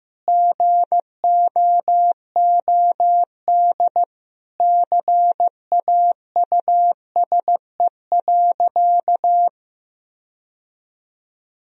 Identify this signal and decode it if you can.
Morse code, heard as a keyed tone, about 15 wpm, 700 Hz. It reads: GOOD CAUSE.
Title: Sentences from Top 1000 Words - Review Part 2 - 15wpm